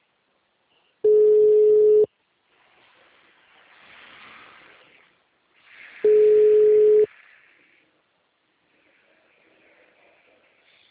Ringback tone (Europe, 425 Hz).
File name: out- -2041- - - .9